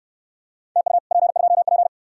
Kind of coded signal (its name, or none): Morse code